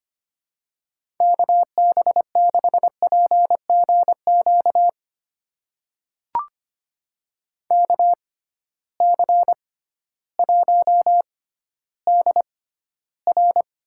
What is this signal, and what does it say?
Morse code, heard as a keyed tone, about 25 wpm, spaced as for 14 wpm, 700 Hz. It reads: KB6PGQ KC1DR
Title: USA Call Signs - 6x Character Spacing - 25wpm